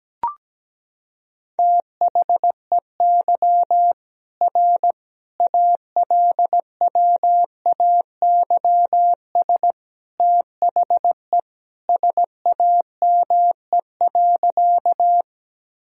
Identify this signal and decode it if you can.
Morse code, heard as a keyed tone, about 17 wpm, 700 Hz. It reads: THEY R ALWAYS THE SAME.
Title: Sentences from Top 300 Words and Abbreviations 17wpm